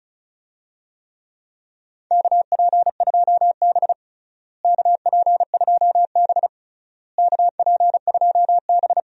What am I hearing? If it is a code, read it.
Morse code, heard as a keyed tone, about 36 wpm, 700 Hz. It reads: KP2B KP2B KP2B
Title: DXCC-100 Call Signs - 4 Characters - Repeated 3x - 35wpm